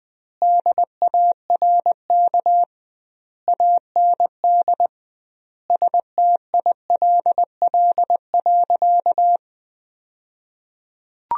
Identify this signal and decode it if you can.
Morse code, heard as a keyed tone, about 20 wpm, 700 Hz. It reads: DARK AND STILL.
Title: 2x Word-Spacing - Sentences from Top 500 Words - 20wpm